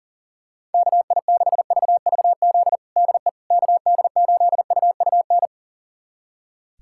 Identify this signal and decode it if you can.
Morse code, heard as a keyed tone, about 39 wpm, 700 Hz. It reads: KI6VVZ DE KD8UUN
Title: USA Callsign DE Callsign - 6 Characters - 40wpm